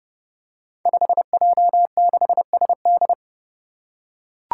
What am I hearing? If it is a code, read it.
Morse code, heard as a keyed tone, about 30 wpm, 700 Hz. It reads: 5J6SD
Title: Five Challenging Letter-Number Combinations 30wpm